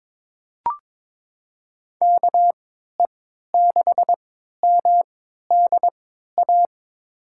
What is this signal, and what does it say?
Morse code, heard as a keyed tone, about 22 wpm, spaced as for 12 wpm, 700 Hz. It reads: KE6MDA
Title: USA Call Signs - 6 Characters - 3x Character Spacing - 22wpm